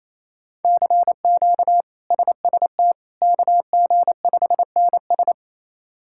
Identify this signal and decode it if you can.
Morse code, heard as a keyed tone, about 28 wpm, 700 Hz. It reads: CQ SST KG5NS
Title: K1USN SST QSO Practice - 28wpm